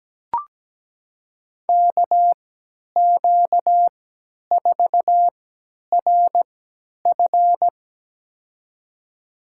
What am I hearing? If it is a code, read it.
Morse code, heard as a keyed tone, about 17 wpm, spaced as for 10 wpm, 700 Hz. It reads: KQ4RF
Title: USA Call Signs - 5 Characters - 3x Character Spacing - 17wpm